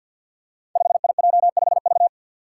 Morse code, heard as a keyed tone, about 52 wpm, 700 Hz. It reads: 5IJ5V
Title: Five Challenging Letter-Number Combinations 50wpm